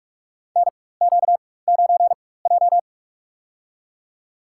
Morse code, heard as a keyed tone, about 46 wpm, spaced as for 21 wpm, 700 Hz. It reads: NQ9J
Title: USA Call Signs - 4 Characters - 4x Character Spacing - 45wpm